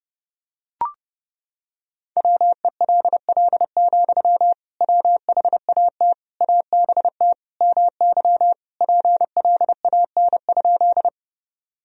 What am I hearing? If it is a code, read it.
Morse code, heard as a keyed tone, about 31 wpm, 700 Hz. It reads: WELL, WHAT ABT MY PLAN?
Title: Sentences from Top 500 Words and Abbreviations 30wpm